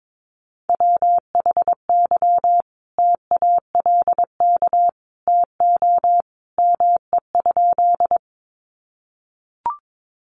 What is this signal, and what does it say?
Morse code, heard as a keyed tone, about 22 wpm, 700 Hz. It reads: WHY TALK TO ME?
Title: Sentences from Top 400 Words 22wpm